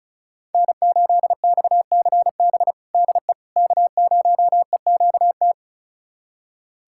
Morse code, heard as a keyed tone, about 36 wpm, 700 Hz. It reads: N8XCB DE K0EQT
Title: USA Callsign DE Callsign - 5 Characters - 35wpm